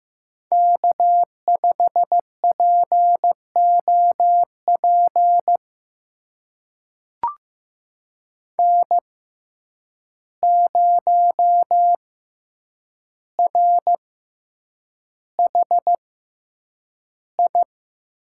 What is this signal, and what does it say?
Morse code, heard as a keyed tone, about 15 wpm, spaced as for 8 wpm, 700 Hz. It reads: K5POP N0RHI
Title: USA Call Signs - 5 Characters - 6x Character Spacing - 15wpm